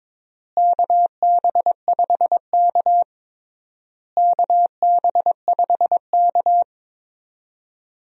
Morse code, heard as a keyed tone, about 22 wpm, 700 Hz. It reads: KB5K KB5K
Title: USA Call Signs - 4 Characters - Repeated 3x - 22wpm